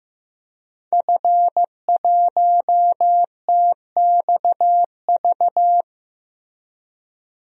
Morse code, heard as a keyed tone, about 15 wpm, 700 Hz. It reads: F1TXV
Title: DXCC-100 Call Signs - 5 Characters - Repeated 2x - 15wpm